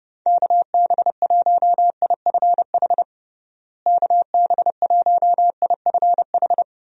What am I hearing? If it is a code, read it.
Morse code, heard as a keyed tone, about 32 wpm, 700 Hz. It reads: KB1IFH KB1IFH
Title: USA Call Signs - 6 Characters - Repeated 3x - 30wpm